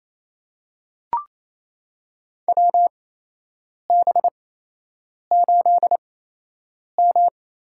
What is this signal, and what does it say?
Morse code, heard as a keyed tone, about 28 wpm, spaced as for 8 wpm, 700 Hz. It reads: WB8M
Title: USA Call Signs - 6 Characters - 8x Character Spacing - 28wpm